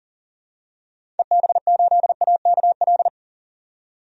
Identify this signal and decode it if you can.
Morse code, heard as a keyed tone, about 40 wpm, 700 Hz. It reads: EB8AKL
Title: DXCC-100 Call Signs - 6 Characters - 40wpm